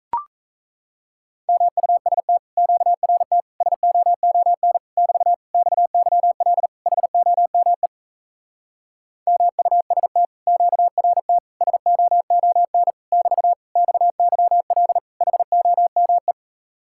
Morse code, heard as a keyed tone, about 37 wpm, 700 Hz. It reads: MUST QRT SOON <BT> XYL HOME MUST QRT SOON <BT> XYL HOME